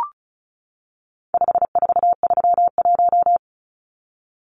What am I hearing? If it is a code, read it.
Morse code, heard as a keyed tone, about 34 wpm, 700 Hz. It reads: 5431
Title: Numbers - 4-Digit 35wpm